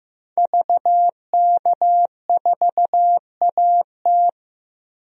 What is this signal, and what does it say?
Morse code, heard as a keyed tone, about 15 wpm, 700 Hz. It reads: VK4AT
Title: DXCC-100 Call Signs - 5 Characters - 15wpm